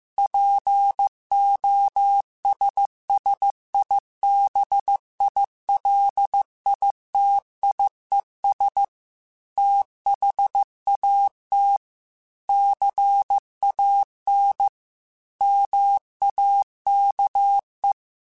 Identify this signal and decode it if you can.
Morse code, heard as a keyed tone, about 15 wpm, 800 Hz. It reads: POSSIBILITIES THAT CAN MAKE
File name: Technical Subjects / Technical Subjects-mp3s / OR Gate @13